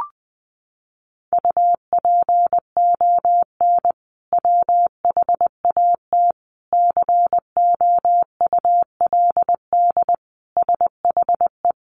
Morse code, heard as a keyed tone, about 20 wpm, 700 Hz. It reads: UPON WHAT COULD SHE
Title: Sentences from Top 1500 Words 20wpm